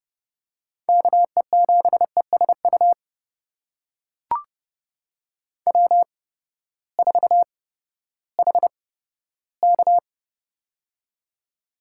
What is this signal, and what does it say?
Morse code, heard as a keyed tone, about 31 wpm, spaced as for 15 wpm, 700 Hz. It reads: KE7ESU W4HK
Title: USA Call Signs - 8x Character Spacing - 30wpm